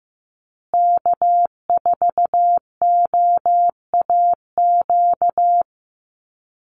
Morse code, heard as a keyed tone, about 15 wpm, 700 Hz. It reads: K4OAQ
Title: Call Signs Repeated 3x - 15wpm